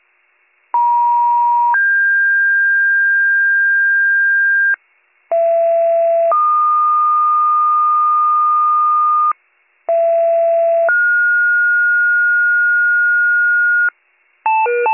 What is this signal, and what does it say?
Two-tone page, heard as a one-second tone, then a three-second tone, about 950 then 1600 Hz, about 670 then 1220 Hz, about 670 then 1430 Hz, with noise. It is heard from 0.7 to 13.9 s.